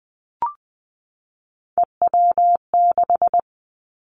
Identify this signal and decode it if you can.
Morse code, heard as a keyed tone, about 20 wpm, 700 Hz. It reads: EW6
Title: Callsign Prefixes for Top 50 DXCC - 20wpm